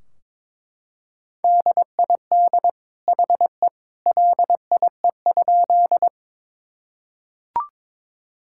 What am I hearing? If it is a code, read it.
Morse code, heard as a keyed tone, about 22 wpm, 700 Hz. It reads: DID HE LIE?
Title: Sentences from Top 1600 Words 22wpm